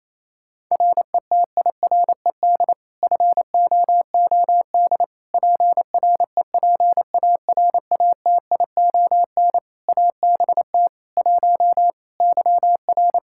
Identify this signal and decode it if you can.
Morse code, heard as a keyed tone, about 27 wpm, 700 Hz. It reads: RETIRED FOOD PREPARATION ABT 1 YR
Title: Practice QSO - Exchange 3 - More Info - 28wpm